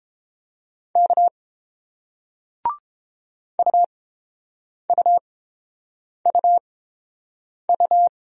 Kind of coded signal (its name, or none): Morse code